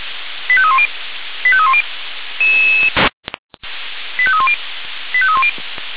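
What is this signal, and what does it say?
Five-tone selective call (ZVEI), heard as quick runs of five tones, tones of about 70 ms. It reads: 85310 85310 85310 85310
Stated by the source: ZVEI five-tone call 85878 received by SDR.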